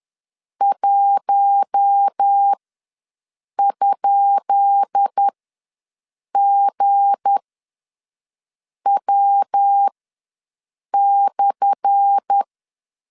Morse code, heard as a keyed tone, about 10 wpm, spaced as for 5 wpm, 800 Hz. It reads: 1?GW/
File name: LEZIONI / 10 SIGNS / 10C DOT ws 10wpm 4213